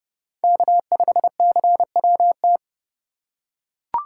Morse code, heard as a keyed tone, about 26 wpm, 700 Hz. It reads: K5CWT